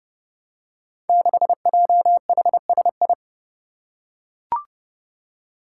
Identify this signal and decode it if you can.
Morse code, heard as a keyed tone, about 29 wpm, 700 Hz. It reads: 6JHSI
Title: Five Challenging Letter-Number Combinations 30wpm